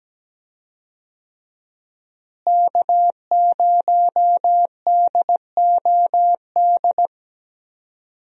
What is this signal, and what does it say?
Morse code, heard as a keyed tone, about 17 wpm, 700 Hz. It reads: K0DOD
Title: USA Call Signs - 5 Characters - Repeated 3x - 17wpm